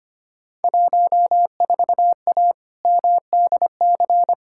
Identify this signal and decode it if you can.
Morse code, heard as a keyed tone, about 25 wpm, 700 Hz. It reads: 14A MDC
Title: Field Day Exchanges Encore - 25wpm